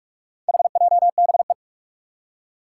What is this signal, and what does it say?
Morse code, heard as a keyed tone, about 41 wpm, 700 Hz. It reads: HJBE
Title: Four Challenging Letter-Number Combinations 45wpm